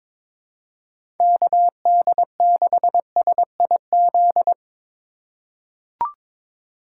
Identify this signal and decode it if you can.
Morse code, heard as a keyed tone, about 23 wpm, 700 Hz. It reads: KD6SIZ
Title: USA Call Signs - 6 Characters - 6x Character Spacing - 22wpm